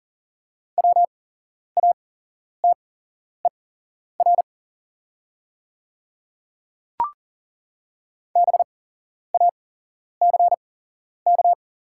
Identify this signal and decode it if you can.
Morse code, heard as a keyed tone, about 40 wpm, spaced as for 11 wpm, 700 Hz. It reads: WATER BACK
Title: 8x Character Spacing - Top 200 Words - 40wpm